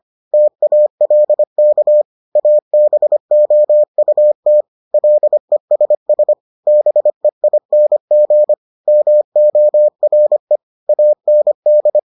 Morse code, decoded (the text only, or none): TALK ABOUT LESS BEING MORE AND